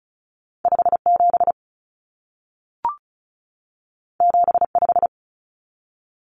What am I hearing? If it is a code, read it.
Morse code, heard as a keyed tone, about 36 wpm, 700 Hz. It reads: 57 75
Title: Common Cut Numbers - 2-Digit 35wpm